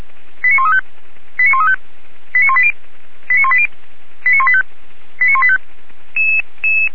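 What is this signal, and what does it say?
Five-tone selective call (ZVEI), heard as quick runs of five tones, tones of about 70 ms. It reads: 89135 89135 89180 89180 89185 89185
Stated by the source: ZVEI five-tone selective call received by SDR, from the feuerwehr-orlamuende.org fire brigade archive.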